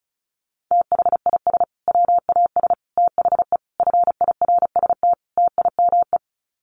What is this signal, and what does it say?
Morse code, heard as a keyed tone, about 34 wpm, 700 Hz. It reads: THIS WAS THE FIRST TIME
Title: Sets of 5 Words Encore 35wpm